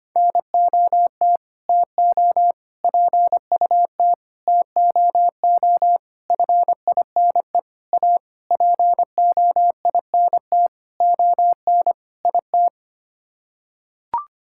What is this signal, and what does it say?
Morse code, heard as a keyed tone, about 25 wpm, 700 Hz. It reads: NOT TO PUT TOO FINE A POINT ON IT
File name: Sets of 9 Words - 25wpm